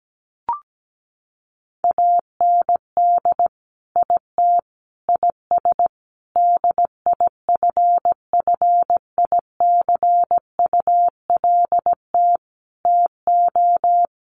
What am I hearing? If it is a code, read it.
Morse code, heard as a keyed tone, about 17 wpm, 700 Hz. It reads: AND IT IS DIFFICULT TO
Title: Sentences from Top 1100 Words 17wpm